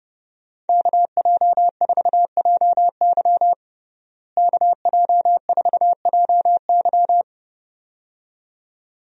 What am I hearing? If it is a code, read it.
Morse code, heard as a keyed tone, about 32 wpm, 700 Hz. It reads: KJ4JY KJ4JY
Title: USA Call Signs - 5 Characters - Repeated 2x - 30wpm